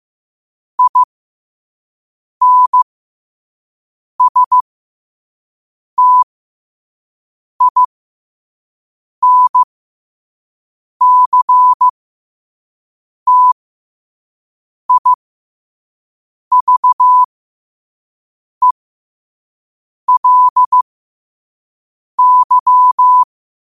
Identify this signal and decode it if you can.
Morse code, heard as a keyed tone, about 15 wpm, spaced as for 5 wpm, 1000 Hz. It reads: INSTINCTIVELY
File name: qotdfiles / 5wpm